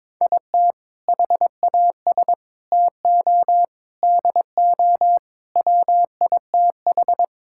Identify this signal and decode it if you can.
Morse code, heard as a keyed tone, about 22 wpm, 700 Hz. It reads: IT HAS TO DO WITH
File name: practice / 22wpm / Sentences with 4-Word Phrases - 22wpm